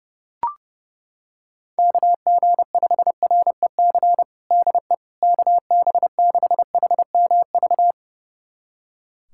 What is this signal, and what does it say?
Morse code, heard as a keyed tone, about 32 wpm, 700 Hz. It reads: KG5REC DE KB6HMV
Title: USA Callsign DE Callsign - 6 Characters - 30wpm